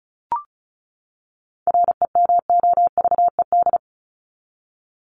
Morse code, heard as a keyed tone, about 34 wpm, 700 Hz. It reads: REMOVED
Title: ED Suffix - Top 100 Words with Base Words 35wpm